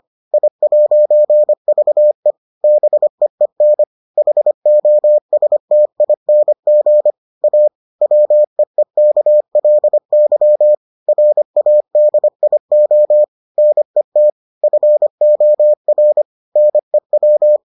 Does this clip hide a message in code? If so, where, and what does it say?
Yes: Morse code from 0.3 to 17.6 s, heard as a keyed tone, about 25 wpm, 600 Hz. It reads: I'VE BEEN HOSTING A WEEKLY RADIO NET FOR NEW